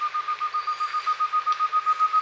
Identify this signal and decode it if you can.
There is no code.